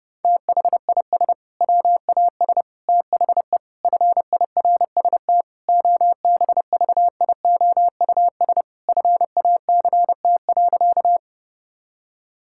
Morse code, heard as a keyed tone, about 30 wpm, 700 Hz. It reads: THIS WAS THE FIRST OBVIOUS FACT.
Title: Sentences from Top 1600 Words 30wpm